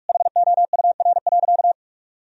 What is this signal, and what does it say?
Morse code, heard as a keyed tone, about 41 wpm, 700 Hz. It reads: HOUR.
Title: Sentences from Top 1000 Words - Review Part 4 - 45wpm